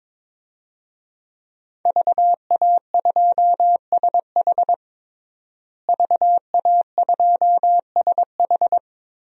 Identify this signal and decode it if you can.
Morse code, heard as a keyed tone, about 22 wpm, 700 Hz. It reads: VA2SH VA2SH
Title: DXCC-100 Call Signs - 5 Characters - Repeated 3x - 22wpm